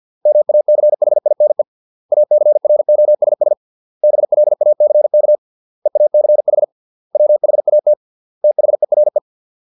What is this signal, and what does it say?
Morse code, heard as a keyed tone, about 50 wpm, 600 Hz. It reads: MACHINE ACROSS BLACK EACH WHAT THERE